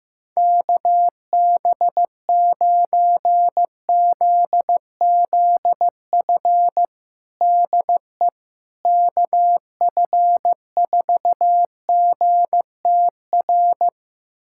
Morse code, heard as a keyed tone, about 16 wpm, 700 Hz. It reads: KB9ZZF DE KF4GTR
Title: USA Callsign DE Callsign - 6 Characters - 15wpm